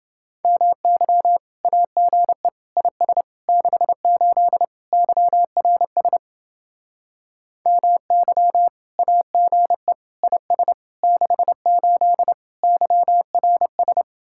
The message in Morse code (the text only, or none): MY AGE IS 68 YRS MY AGE IS 68 YRS